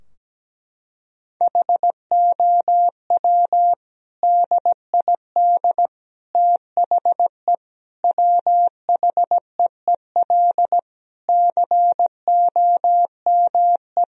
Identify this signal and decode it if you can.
Morse code, heard as a keyed tone, about 17 wpm, 700 Hz. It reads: HOW DID THE WHEEL COME
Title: Sentences from Top 1000 Words - Review Part 3 - 17wpm